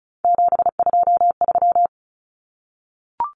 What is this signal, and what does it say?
Morse code, heard as a keyed tone, about 34 wpm, 700 Hz. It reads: 723
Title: Numbers - 3-Digit 35wpm